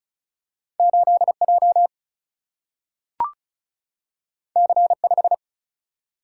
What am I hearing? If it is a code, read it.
Morse code, heard as a keyed tone, about 36 wpm, 700 Hz. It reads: 8J C5
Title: Single-Letter Single-Number 35wpm